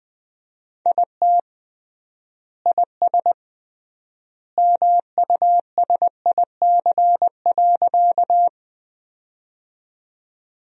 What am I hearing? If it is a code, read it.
Morse code, heard as a keyed tone, about 20 wpm, 700 Hz. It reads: IT IS MUSIC.
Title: 3x Word-Spacing - Sentences from Top 300 Words - 20wpm